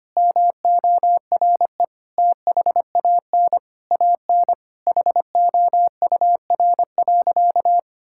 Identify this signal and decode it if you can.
Morse code, heard as a keyed tone, about 25 wpm, 700 Hz. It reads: MORE THAN AN HOUR.
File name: practice / 25wpm / Sentences with 3-Word Phrases - 25wpm